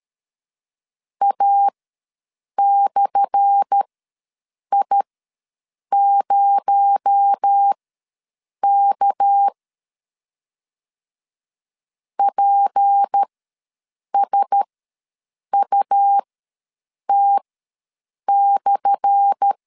Morse code, heard as a keyed tone, about 13 wpm, spaced as for 7 wpm, 800 Hz. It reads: A/I0K PSUT/
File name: LEZIONI / 10 SIGNS / 10A SLASH ws 12wpm 3443